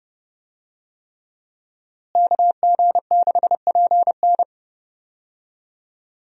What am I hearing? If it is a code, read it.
Morse code, heard as a keyed tone, about 32 wpm, 700 Hz. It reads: KG6PN
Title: USA Call Signs - 5 Characters - Repeated 3x - 30wpm